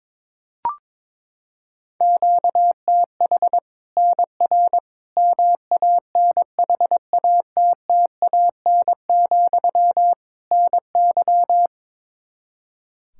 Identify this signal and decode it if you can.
Morse code, heard as a keyed tone, about 21 wpm, 700 Hz. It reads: QTH NR MANHATTAN, NY
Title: Practice QSO - Exchange 1 - Beginning - 22wpm